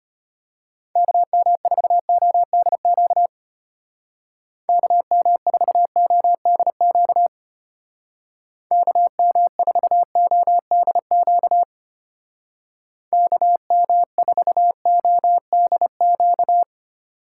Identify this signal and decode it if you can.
Morse code, heard as a keyed tone, about 33 wpm, 700 Hz. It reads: KM4ODQ KM4ODQ KM4ODQ KM4ODQ